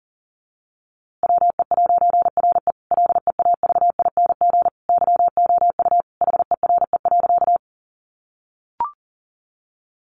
Morse code, heard as a keyed tone, about 40 wpm, 700 Hz. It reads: WE'RE LEAVING YOU HERE.
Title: A Sound Of Thunder 40wpm